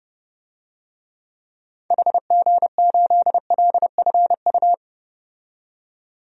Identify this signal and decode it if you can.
Morse code, heard as a keyed tone, about 30 wpm, 700 Hz. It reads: HG8LFU
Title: DXCC-100 Call Signs - 6 Characters - 30wpm